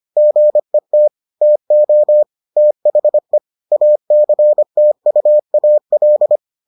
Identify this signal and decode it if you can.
Morse code, heard as a keyed tone, about 25 wpm, 600 Hz. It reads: GET TO THE ACTUAL